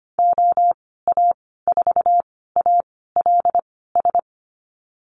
Morse code, heard as a keyed tone, about 25 wpm, spaced as for 16 wpm, 700 Hz. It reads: OA4ALS
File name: Lesson 15g - Number 5 - Call Signs 25wpm